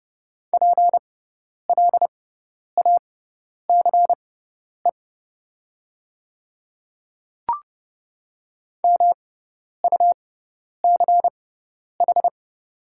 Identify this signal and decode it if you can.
Morse code, heard as a keyed tone, about 30 wpm, spaced as for 10 wpm, 690 Hz. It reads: PLACE MUCH
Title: 6x Character Spacing - Top 200 Words - 30wpm